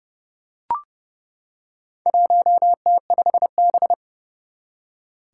Morse code, heard as a keyed tone, about 30 wpm, 700 Hz. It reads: 1T5B
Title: Four Challenging Letter-Number Combinations 30wpm